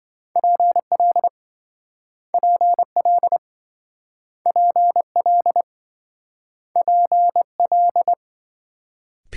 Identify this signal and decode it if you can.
Morse code, heard as a keyed tone, about 27 wpm, 700 Hz. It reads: PL PL PL PL